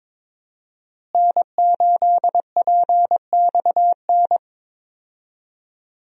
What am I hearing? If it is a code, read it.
Morse code, heard as a keyed tone, about 22 wpm, 700 Hz. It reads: N8PXN
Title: USA Call Signs - 5 Characters - Repeated 3x - 22wpm